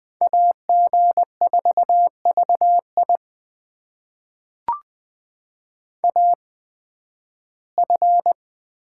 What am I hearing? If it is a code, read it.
Morse code, heard as a keyed tone, about 20 wpm, spaced as for 13 wpm, 700 Hz. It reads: AG4VI AF
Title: USA Call Signs - 5 Characters - 8x Character Spacing - 20wpm